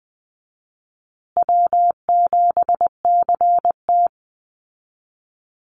Morse code, heard as a keyed tone, about 20 wpm, 700 Hz. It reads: W7CT
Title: Easy Call Signs Repeated 3x - 20wpm